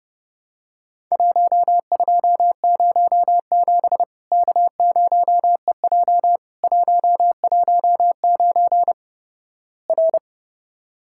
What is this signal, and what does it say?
Morse code, heard as a keyed tone, about 32 wpm, 700 Hz. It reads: 1207 K0EJ 119 R
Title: Worked All Europe DX Contest - QTC Exchange - 30wpm